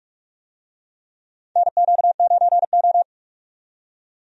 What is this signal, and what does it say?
Morse code, heard as a keyed tone, about 46 wpm, 700 Hz. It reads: NQ9O
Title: USA Call Signs - 4 Characters - 45wpm